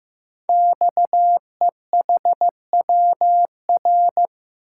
Morse code, heard as a keyed tone, about 15 wpm, 700 Hz. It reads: XEHWR
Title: Code Groups - 5 Letters - 15wpm